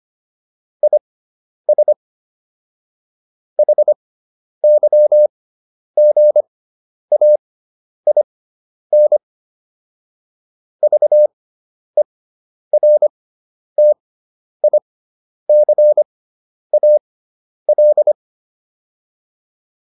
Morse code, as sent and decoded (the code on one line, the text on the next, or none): .. ... / .... -.-- --. .- .. -. / ...- . .-. - .. -.-. .- .-..
IS HYGAIN VERTICAL